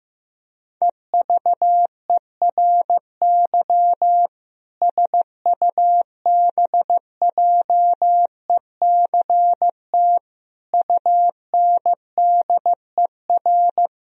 Morse code, heard as a keyed tone, about 15 wpm, 700 Hz. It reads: EVERY SUBJECT UNDER
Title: Sentences from Top 1000 Words - Review Part 1 - 15wpm